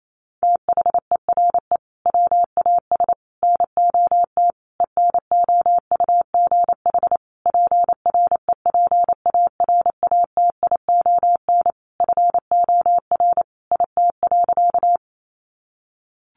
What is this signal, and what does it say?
Morse code, heard as a keyed tone, about 28 wpm, 700 Hz. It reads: THERE WAS NOT ENOUGH PREPARATION FOR IT.
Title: Sentences from Top 1400 Words 28wpm